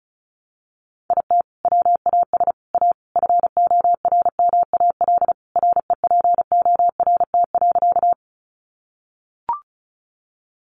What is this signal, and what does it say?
Morse code, heard as a keyed tone, about 34 wpm, 700 Hz. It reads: IT WAS A FORMAL REPORT.
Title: Sentences from Top 1300 Words 35wpm